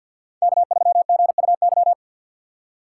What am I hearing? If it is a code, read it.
Morse code, heard as a keyed tone, about 52 wpm, 700 Hz. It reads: K3GUY
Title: USA Call Signs - 2x Character Spacing - 50wpm